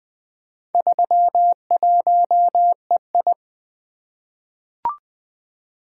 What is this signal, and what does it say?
Morse code, heard as a keyed tone, about 20 wpm, 700 Hz. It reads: 31EI
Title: Four Letter-Number Combinations - 20wpm